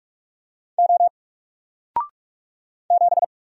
Morse code, heard as a keyed tone, about 46 wpm, 700 Hz. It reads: O 7
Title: Single Letter-Number Alt 45wpm